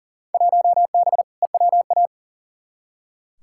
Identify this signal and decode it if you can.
Morse code, heard as a keyed tone, about 37 wpm, 700 Hz. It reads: 1B EWA
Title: Field Day Exchanges Encore - 40wpm